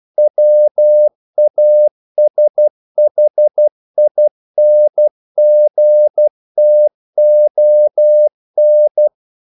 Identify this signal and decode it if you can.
Morse code, heard as a keyed tone, about 12 wpm, 600 Hz. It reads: WASHINGTON
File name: CW / Practice / NationalCapitals12